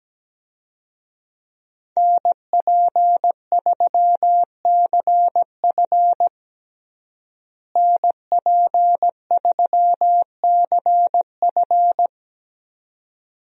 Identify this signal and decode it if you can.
Morse code, heard as a keyed tone, about 17 wpm, 700 Hz. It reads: NP3CF NP3CF
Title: USA Call Signs - 5 Characters - Repeated 2x - 17wpm